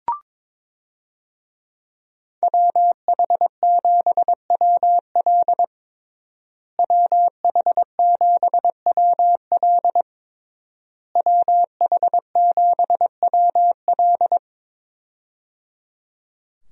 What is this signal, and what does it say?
Morse code, heard as a keyed tone, about 22 wpm, 700 Hz. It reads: WH7WL WH7WL WH7WL
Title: USA Call Signs - 5 Characters - Repeated 3x - 22wpm